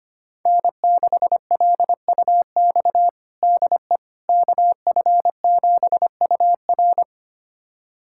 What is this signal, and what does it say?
Morse code, heard as a keyed tone, about 25 wpm, 700 Hz. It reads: N6LUX DE KF7UR
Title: USA Callsign DE Callsign - 5 Characters - 25wpm